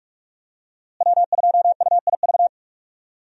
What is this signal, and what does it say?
Morse code, heard as a keyed tone, about 46 wpm, 700 Hz. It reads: W2UIV
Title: USA Call Signs - 5 Characters - Repeated 3x - 45wpm